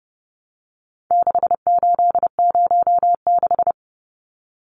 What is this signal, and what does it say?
Morse code, heard as a keyed tone, about 28 wpm, 700 Hz. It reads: 6806
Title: Numbers - 4-Digit 30wpm